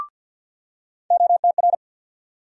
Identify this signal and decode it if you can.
Morse code, heard as a keyed tone, about 52 wpm, 700 Hz. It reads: OTR
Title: USA Call Sign Postfixes - 3 Letters 50wpm